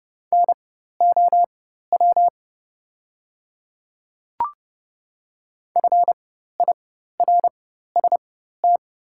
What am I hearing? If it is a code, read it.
Morse code, heard as a keyed tone, about 30 wpm, spaced as for 14 wpm, 700 Hz. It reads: NOW FIRST
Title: 4x Character Spacing - Top 100 Words - 30wpm